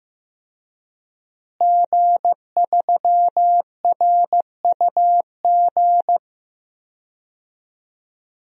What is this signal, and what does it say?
Morse code, heard as a keyed tone, about 15 wpm, 700 Hz. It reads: G3RUG
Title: DXCC-100 Call Signs - 5 Characters - Repeated 2x - 15wpm